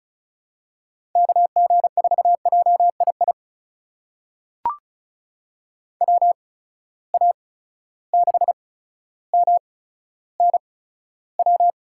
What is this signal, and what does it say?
Morse code, heard as a keyed tone, about 36 wpm, spaced as for 15 wpm, 700 Hz. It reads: KG4JII WA6MNW
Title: USA Call Signs - 6 Characters - 8x Character Spacing - 35wpm